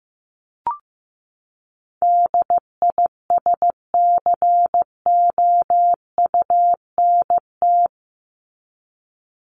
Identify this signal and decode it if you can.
Morse code, heard as a keyed tone, about 15 wpm, 700 Hz. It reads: DISCOUNT
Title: Top 1001-1250 Words 15wpm